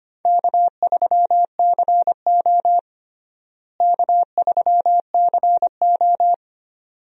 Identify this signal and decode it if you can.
Morse code, heard as a keyed tone, about 25 wpm, 700 Hz. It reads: K3CO K3CO
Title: USA Call Signs - 4 Characters - Repeated 2x - 25wpm